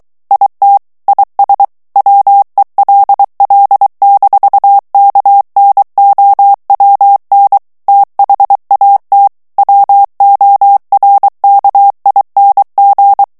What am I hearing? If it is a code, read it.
Morse code, heard as a keyed tone, about 23 wpm, 800 Hz. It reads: IT IS WELL-KNOWN THAT WORKING
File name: morse mp3 / ANGLAIS texte clair 1 vitesse 22 mots minute